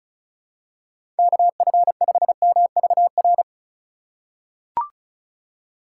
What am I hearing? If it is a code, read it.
Morse code, heard as a keyed tone, about 36 wpm, 700 Hz. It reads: KF5MVR
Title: USA Call Signs - 6 Characters - 3x Character Spacing - 35wpm